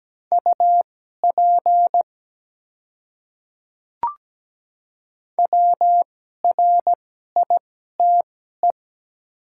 Morse code, heard as a keyed tone, about 17 wpm, spaced as for 12 wpm, 700 Hz. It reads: UP WRITE
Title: 2x Character Spacing - Top 100 Words - 17wpm